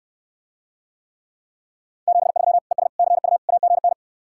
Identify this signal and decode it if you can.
Morse code, heard as a keyed tone, about 69 wpm, 700 Hz. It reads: 73 ES CU AGN